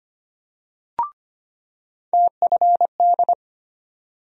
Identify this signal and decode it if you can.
Morse code, heard as a keyed tone, about 25 wpm, 700 Hz. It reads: TFD
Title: USA Call Sign Postfixes - 3 Letters 25wpm